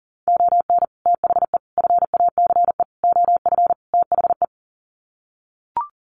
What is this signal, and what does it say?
Morse code, heard as a keyed tone, about 40 wpm, 700 Hz. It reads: ON THE FACE OF THE
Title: Sets of 5 Words Encore2 40wpm